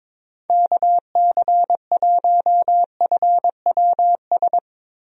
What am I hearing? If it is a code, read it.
Morse code, heard as a keyed tone, about 22 wpm, 700 Hz. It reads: KC1FWS
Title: USA Call Signs - 6 Characters - Repeated 2x - 22wpm